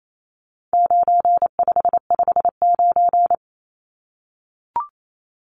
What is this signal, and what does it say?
Morse code, heard as a keyed tone, about 28 wpm, 700 Hz. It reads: 9559
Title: Numbers - 4-Digit 28wpm